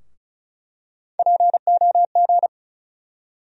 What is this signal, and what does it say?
Morse code, heard as a keyed tone, about 35 wpm, 700 Hz. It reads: POG
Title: USA Call Sign Postfixes - 3 Letters 35wpm